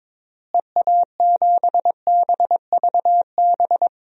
Morse code, heard as a keyed tone, about 22 wpm, 700 Hz. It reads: EA7BVB
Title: DXCC-100 Call Signs - 6 Characters - Repeated 2x - 22wpm